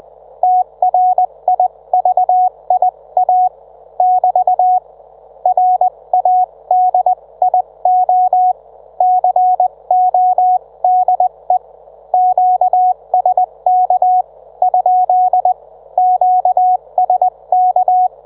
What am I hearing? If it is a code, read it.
Morse code, heard as a keyed tone, about 20 wpm, spaced as for 18 wpm, 730 Hz. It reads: TRIVIA <BT> RADIO CODE QSK ? QSK